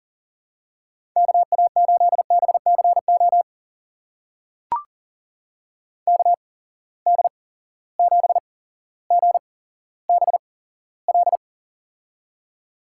Morse code, heard as a keyed tone, about 40 wpm, spaced as for 17 wpm, 700 Hz. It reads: KA8BCO KD7GBL
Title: USA Call Signs - 6 Characters - 8x Character Spacing - 40wpm